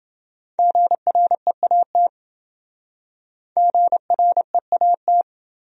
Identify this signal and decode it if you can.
Morse code, heard as a keyed tone, about 26 wpm, 700 Hz. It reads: GREAT GREAT